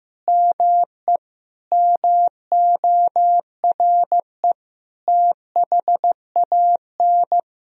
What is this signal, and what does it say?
Morse code, heard as a keyed tone, about 15 wpm, 700 Hz. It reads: ME MORE THAN
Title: Sentences from Top 200 Words and Abbreviations 15wpm